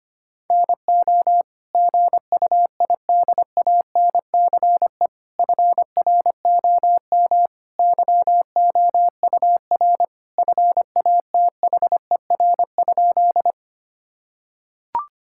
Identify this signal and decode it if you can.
Morse code, heard as a keyed tone, about 25 wpm, 700 Hz. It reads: NO GUIDANCE FROM YOUR FATHER?
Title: Sentences from Top 1600 Words 25wpm